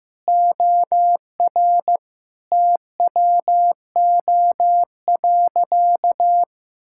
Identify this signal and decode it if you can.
Morse code, heard as a keyed tone, about 15 wpm, 700 Hz. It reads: OR TWO.
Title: Sentences from Top 1000 Words - Review Part 5 - 15wpm